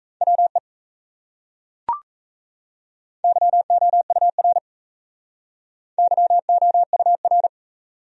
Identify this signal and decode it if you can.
Morse code, heard as a keyed tone, about 37 wpm, 700 Hz. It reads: WE YOUR YOUR